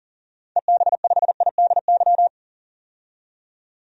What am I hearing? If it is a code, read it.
Morse code, heard as a keyed tone, about 38 wpm, 700 Hz. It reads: EB5IDY
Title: DXCC-100 Call Signs - 6 Characters - 40wpm